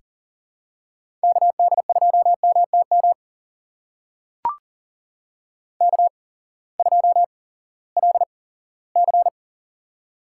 Morse code, heard as a keyed tone, about 40 wpm, spaced as for 20 wpm, 700 Hz. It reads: KD2MTM K2LC
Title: USA Call Signs - 8x Character Spacing - 40wpm